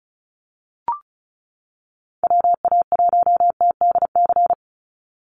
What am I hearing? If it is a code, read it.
Morse code, heard as a keyed tone, about 34 wpm, 700 Hz. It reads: WA1TDC
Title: USA Call Signs 35wpm